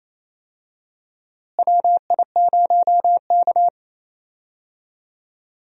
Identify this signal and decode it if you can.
Morse code, heard as a keyed tone, about 28 wpm, 700 Hz. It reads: WI0K
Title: USA Call Signs - 4 Characters - Repeated 2x - 28wpm